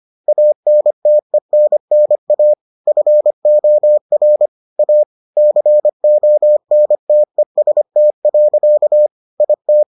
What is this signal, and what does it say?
Morse code, heard as a keyed tone, about 25 wpm, 600 Hz. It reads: ANTENNA FOR A CONTEST. IT